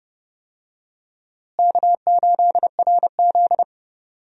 Morse code, heard as a keyed tone, about 30 wpm, 700 Hz. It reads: K8RZ
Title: USA Call Signs - 4 Characters - Repeated 2x - 30wpm